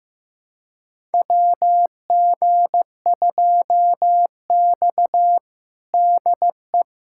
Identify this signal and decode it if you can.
Morse code, heard as a keyed tone, about 15 wpm, 700 Hz. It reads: WG2X DE
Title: USA Callsign DE Callsign - 4 Characters - 15wpm